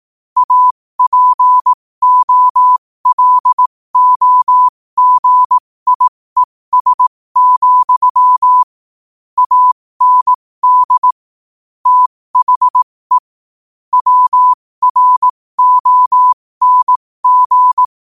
Morse code, heard as a keyed tone, about 18 wpm, spaced as for 15 wpm, 1000 Hz. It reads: APOLOGIES, AND THE WRONG
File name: qotdfiles / 15wpm